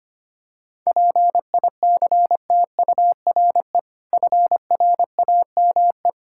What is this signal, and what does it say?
Morse code, heard as a keyed tone, about 25 wpm, 700 Hz. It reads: PICTURE FRAME
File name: practice / 25wpm / Difficult Words with Hints - 25wpm